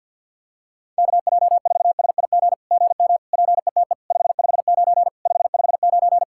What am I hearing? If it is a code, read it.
Morse code, heard as a keyed tone, about 50 wpm, 700 Hz. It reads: KJ4SIG GM PETE 559 559
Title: SOTA QSO Practice 50wpm